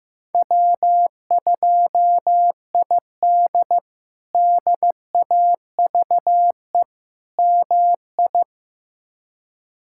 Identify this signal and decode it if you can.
Morse code, heard as a keyed tone, about 15 wpm, 700 Hz. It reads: W2ID DAVE MI
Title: K1USN SST QSO Practice - 15wpm